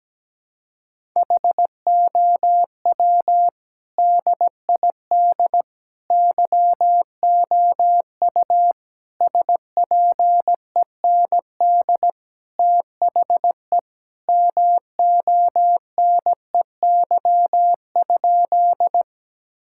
Morse code, heard as a keyed tone, about 17 wpm, 700 Hz. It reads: HOW DID YOU SPEND THE MONEY?
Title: Sentences from Top 1000 Words - Review Part 5 - 17wpm